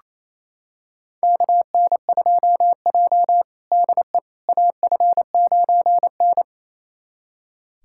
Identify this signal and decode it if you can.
Morse code, heard as a keyed tone, about 28 wpm, 700 Hz. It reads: KN2J DE AF9N
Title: USA Callsign DE Callsign - 4 Characters - 28wpm